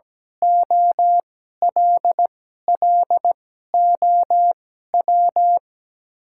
Morse code, heard as a keyed tone, about 17 wpm, spaced as for 12 wpm, 700 Hz. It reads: OLLOW
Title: 2x Character Spacing - Top 300 Words - 17wpm